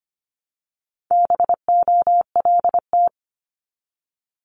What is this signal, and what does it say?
Morse code, heard as a keyed tone, about 25 wpm, 700 Hz. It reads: BOLT
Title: Multi-letter Phonemes 25wpm